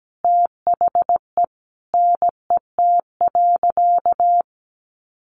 Morse code, heard as a keyed tone, about 17 wpm, 700 Hz. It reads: THE NET.
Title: Sentences from Top 1500 Words 17wpm